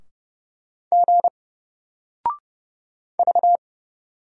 Morse code, heard as a keyed tone, about 30 wpm, 700 Hz. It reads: G V
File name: Single-Letter-Number-30wpm